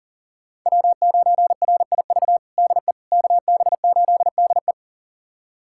Morse code, heard as a keyed tone, about 40 wpm, 700 Hz. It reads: W9RIV DE KB8DE
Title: USA Callsign DE Callsign - 5 Characters - 40wpm